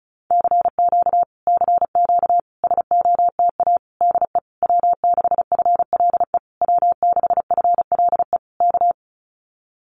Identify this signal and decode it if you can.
Morse code, heard as a keyed tone, about 34 wpm, 700 Hz. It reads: CQ CQ SOTA DE W6FLE W6FLE K
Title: Common QSO Phrases Encore 35wpm